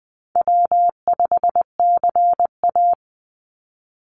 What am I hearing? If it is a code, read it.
Morse code, heard as a keyed tone, about 20 wpm, 700 Hz. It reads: W5CA
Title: USA Call Signs 20wpm